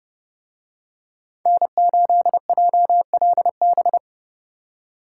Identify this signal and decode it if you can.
Morse code, heard as a keyed tone, about 28 wpm, 700 Hz. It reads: N8JLB